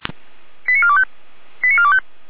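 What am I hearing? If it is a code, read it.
Five-tone selective call (ZVEI), heard as quick runs of five tones, tones of about 70 ms. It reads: 89425 89425
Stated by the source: ZVEI five-tone selective call received on an SDR.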